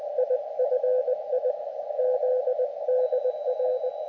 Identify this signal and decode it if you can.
Morse code, heard as a keyed tone, about 20 wpm, 510 Hz, with radio noise. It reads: IFI ZDR
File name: Expedition / SWL UT5NM 20.12.2012 VANGUARD UA1CEG p IT9IFI